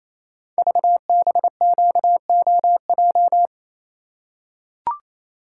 Morse code, heard as a keyed tone, about 28 wpm, 700 Hz. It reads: VBQOJ
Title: Code Groups - 5 Letters - 28wpm